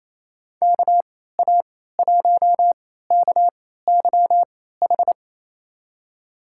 Morse code, heard as a keyed tone, about 28 wpm, spaced as for 16 wpm, 700 Hz. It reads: KA1KYH
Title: USA Call Signs - 3x Character Spacing - 28wpm